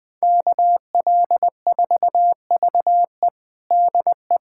Morse code, heard as a keyed tone, about 20 wpm, 700 Hz. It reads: KL4VE DE
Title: USA Callsign DE Callsign - 5 Characters - 20wpm